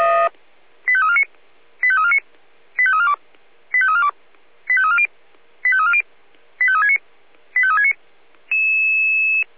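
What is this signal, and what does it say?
Five-tone selective call (ZVEI), heard as quick runs of five tones, tones of about 70 ms. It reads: 85349 85349 85342 85342 85340 85340 85369 85369
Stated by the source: ZVEI five-tone selective call 85360, received on SDR.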